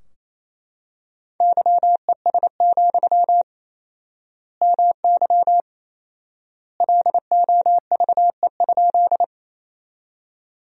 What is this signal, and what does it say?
Morse code, heard as a keyed tone, about 28 wpm, 700 Hz. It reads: YES, MY LOVE?
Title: 4x Word-Spacing - Sentences from Top 500 Words - 28wpm